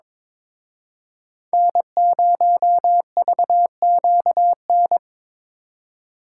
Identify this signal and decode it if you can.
Morse code, heard as a keyed tone, about 26 wpm, 700 Hz. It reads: N0VQN
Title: Speed Racing - USA Call Signs 22wpm